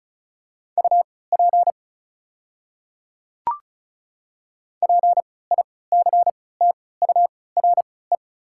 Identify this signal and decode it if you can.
Morse code, heard as a keyed tone, about 34 wpm, spaced as for 20 wpm, 700 Hz. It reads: UP PICTURE
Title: 3x Character Spacing - Top 300 Words - 35wpm